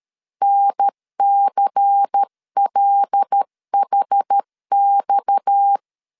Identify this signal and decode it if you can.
Morse code, heard as a keyed tone, about 13 wpm, 800 Hz. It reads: NCLHX
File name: CW / DET / 1 6 1500C S C 12wpm 3533